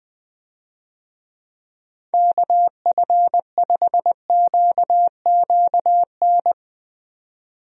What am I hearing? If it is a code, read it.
Morse code, heard as a keyed tone, about 20 wpm, 700 Hz. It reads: KF5QQN
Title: USA Call Signs - 6 Characters - 20wpm